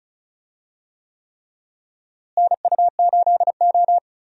Morse code, heard as a keyed tone, about 36 wpm, 700 Hz. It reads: NU8O
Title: USA Call Signs - 4 Characters - Repeated 3x - 35wpm